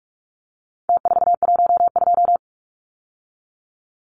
Morse code, heard as a keyed tone, about 44 wpm, 700 Hz. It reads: T412
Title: Numbers - Contest Style - 4-Digit 45wpm